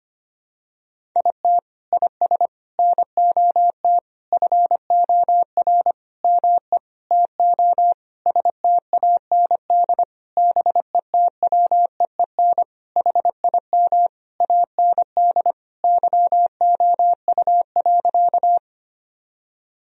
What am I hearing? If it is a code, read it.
Morse code, heard as a keyed tone, about 25 wpm, 700 Hz. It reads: IT IS NOT FOR ME TO STAND BETWEEN HIM AND YOU.
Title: Sentences from Top 1000 Words - Review Part 1 - 25wpm